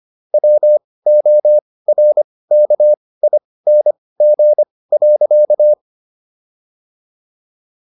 Morse code, heard as a keyed tone, about 25 wpm, spaced as for 18 wpm, 600 Hz. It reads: WORKING.